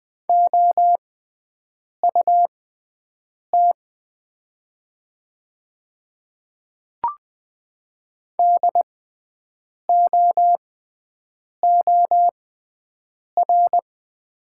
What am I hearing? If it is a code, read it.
Morse code, heard as a keyed tone, about 20 wpm, spaced as for 7 wpm, 700 Hz. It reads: OUT DOOR
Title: 6x Character Spacing - Top 500 Words - 20wpm